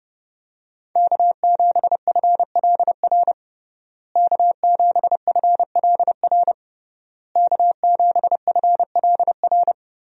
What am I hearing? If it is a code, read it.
Morse code, heard as a keyed tone, about 30 wpm, 700 Hz. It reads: K7FLR K7FLR K7FLR
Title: USA Call Signs - 5 Characters - Repeated 3x - 30wpm